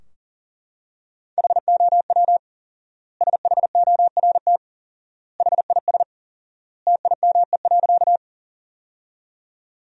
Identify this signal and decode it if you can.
Morse code, heard as a keyed tone, about 40 wpm, 700 Hz. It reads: HOW SHORT HIS TIME.
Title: 4x Word-Spacing - Sentences from Top 400 Words - 40wpm